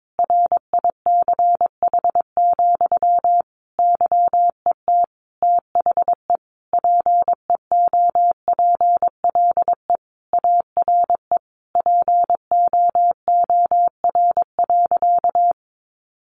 Morse code, heard as a keyed tone, about 22 wpm, 700 Hz. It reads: RICH, YET THE PEOPLE ARE POOR.
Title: Sentences from Top 1100 Words 22wpm